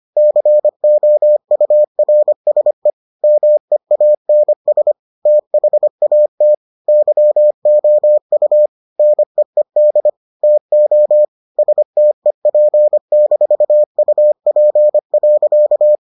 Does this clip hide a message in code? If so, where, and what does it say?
Yes: Morse code from 0.2 to 16.0 s, heard as a keyed tone, about 25 wpm, 600 Hz. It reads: COURSE MEANS THAT YOU NEED TO STEP-UP.